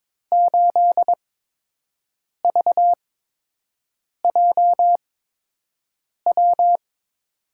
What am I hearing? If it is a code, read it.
Morse code, heard as a keyed tone, about 22 wpm, spaced as for 6 wpm, 700 Hz. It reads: 8VJW
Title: USA Call Signs - 8x Character Spacing - 22wpm